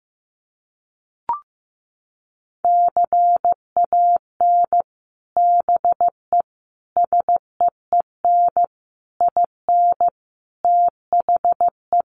Morse code, heard as a keyed tone, about 15 wpm, 700 Hz. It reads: CAN BE SEEN IN THE
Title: Sets of 5 Words Encore2 15wpm